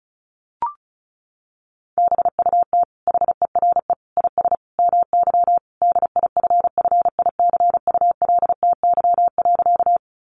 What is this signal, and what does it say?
Morse code, heard as a keyed tone, about 34 wpm, 700 Hz. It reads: BUT HERE IS MY DIFFICULTY.
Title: Sentences from Top 1500 Words 35wpm